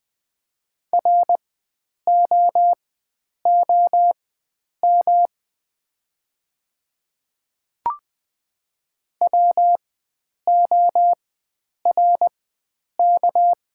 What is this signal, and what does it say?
Morse code, heard as a keyed tone, about 20 wpm, spaced as for 9 wpm, 690 Hz. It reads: ROOM WORK
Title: 4x Character Spacing - Top 500 Words - 20wpm